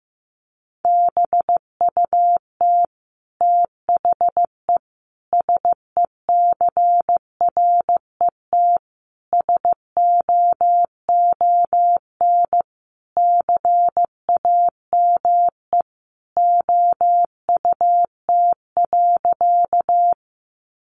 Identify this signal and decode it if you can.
Morse code, heard as a keyed tone, about 15 wpm, 700 Hz. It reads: BUT THE SECRET SOON CAME OUT.
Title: Sentences from Top 1200 Words 15wpm